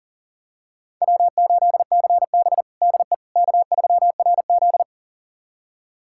Morse code, heard as a keyed tone, about 37 wpm, 700 Hz. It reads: W8CB DE K3RZ
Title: USA Callsign DE Callsign - 4 Characters - 40wpm